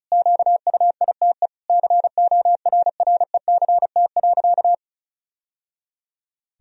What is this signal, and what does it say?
Morse code, heard as a keyed tone, about 34 wpm, 700 Hz. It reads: QUITE CORRECT.
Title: Sentences from Top 1000 Words - Review Part 2 - 35wpm